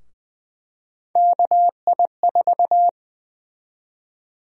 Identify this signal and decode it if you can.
Morse code, heard as a keyed tone, about 21 wpm, 700 Hz. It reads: KI4
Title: USA Call Sign Prefixes - Repeated 2x - 20wpm